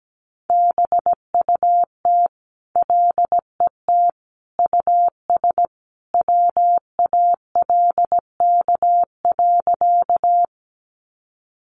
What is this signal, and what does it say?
Morse code, heard as a keyed tone, about 17 wpm, 700 Hz. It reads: BUT LET US WALK.
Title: Sentences from Top 300 Words 17wpm